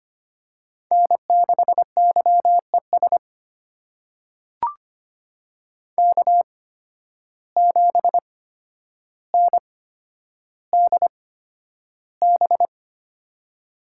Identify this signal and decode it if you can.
Morse code, heard as a keyed tone, about 25 wpm, spaced as for 11 wpm, 700 Hz. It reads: N6YES K7NDB
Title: USA Call Signs - 8x Character Spacing - 25wpm